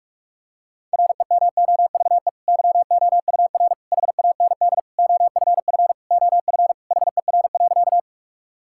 Morse code, heard as a keyed tone, about 46 wpm, 700 Hz. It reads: REMOVE YOUR HAND OFF OF HER.
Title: Sentences from Top 1000 Words - Review Part 5 - 45wpm